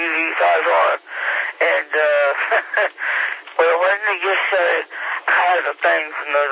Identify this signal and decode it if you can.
Emergency Alert System (SAME) tones (header bursts), heard as harsh buzzing bursts, with speech. It is heard from 1.0 to 5.4 s.